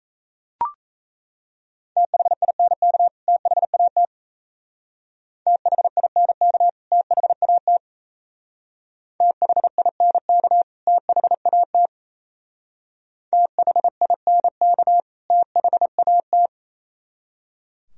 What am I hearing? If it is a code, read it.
Morse code, heard as a keyed tone, about 34 wpm, 700 Hz. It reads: THINK THAT THINK THAT THINK THAT THINK THAT
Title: Speed-Racing - Sets of 2 Words 28wpm